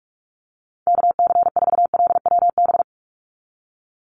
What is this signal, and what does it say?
Morse code, heard as a keyed tone, about 44 wpm, 700 Hz. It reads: KC4LWB
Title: USA Call Signs 45wpm